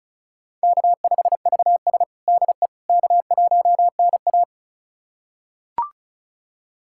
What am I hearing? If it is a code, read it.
Morse code, heard as a keyed tone, about 36 wpm, 700 Hz. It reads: K5VS DE K1NA
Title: USA Callsign DE Callsign - 4 Characters - 35wpm